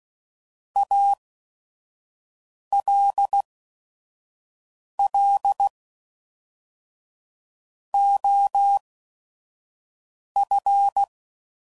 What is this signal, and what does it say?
Morse code, heard as a keyed tone, about 16 wpm, spaced as for 5 wpm, 800 Hz. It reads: ALL OF
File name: CW Code Groups @5WPM